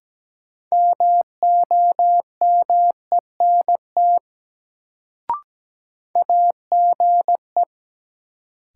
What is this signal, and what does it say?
Morse code, heard as a keyed tone, about 17 wpm, 700 Hz. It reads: MOMENT AGE
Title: Sets of 3 Words - New Word Drill - 17wpm